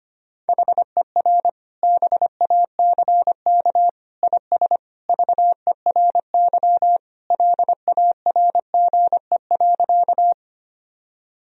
Morse code, heard as a keyed tone, about 25 wpm, 700 Hz. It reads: HER BACK IS VERY LARGE.
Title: Sentences from Top 1000 Words - Review Part 1 - 25wpm